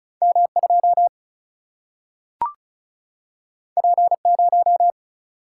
Morse code, heard as a keyed tone, about 36 wpm, 700 Hz. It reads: M2 P0
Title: Single-Letter Single-Number 35wpm